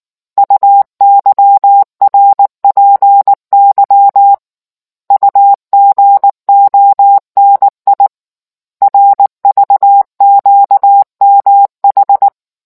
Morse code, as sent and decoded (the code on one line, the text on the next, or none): ..- -.-- .-. .--. -.-- / ..- --. --- -. .. / .-. ...- --.- -- ....
UYRPY UGONI RVQMH